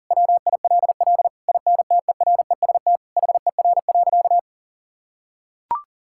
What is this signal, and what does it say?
Morse code, heard as a keyed tone, about 40 wpm, 700 Hz. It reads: WILL INTEREST HER.
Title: Sentences from Top 1000 Words - Review Part 5 - 40wpm